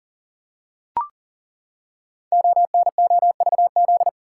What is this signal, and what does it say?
Morse code, heard as a keyed tone, about 38 wpm, 700 Hz. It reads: ONOVZ
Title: Code Groups - 5 Letters - 40wpm